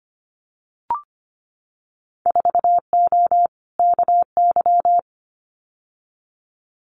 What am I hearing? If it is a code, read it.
Morse code, heard as a keyed tone, about 25 wpm, 700 Hz. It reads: 4O KY
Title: Winter Field Day Exchanges Encore 25wpm